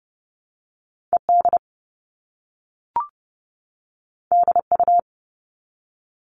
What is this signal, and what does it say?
Morse code, heard as a keyed tone, about 30 wpm, 700 Hz. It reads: ED DU
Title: All Cut Numbers - 2-Digit 30wpm